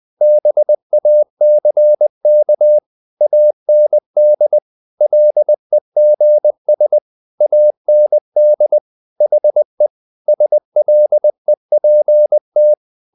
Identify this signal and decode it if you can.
Morse code, heard as a keyed tone, about 20 wpm, 600 Hz. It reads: BACK AND LEGS AND HE SLEPT
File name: the old man and the sea part 2 20wpm